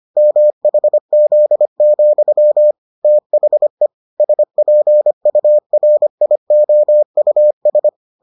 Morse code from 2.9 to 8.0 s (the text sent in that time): THE SPURIOUS